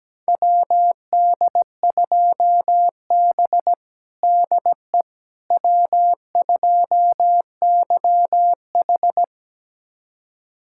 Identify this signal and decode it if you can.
Morse code, heard as a keyed tone, about 17 wpm, 700 Hz. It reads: WD2B DE W2YH
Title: USA Callsign DE Callsign - 4 Characters - 17wpm